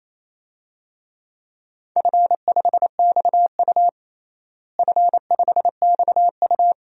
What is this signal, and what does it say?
Morse code, heard as a keyed tone, about 28 wpm, 700 Hz. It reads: F5XU F5XU
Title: DXCC-100 Call Signs - 4 Characters - Repeated 2x - 28wpm